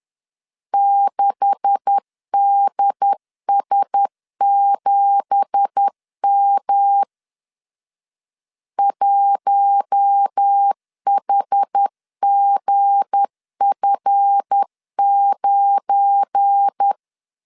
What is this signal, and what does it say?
Morse code, heard as a keyed tone, about 10 wpm, 800 Hz. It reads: 6DS7M 1HGF9
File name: CW / DET / 1 9 1500C S A 10wpm 4511